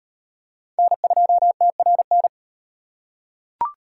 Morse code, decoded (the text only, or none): N2TRN